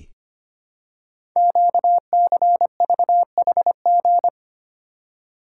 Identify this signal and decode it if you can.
Morse code, heard as a keyed tone, about 25 wpm, 700 Hz. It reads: QCVHG
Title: Code Groups - 5 Letters - 25wpm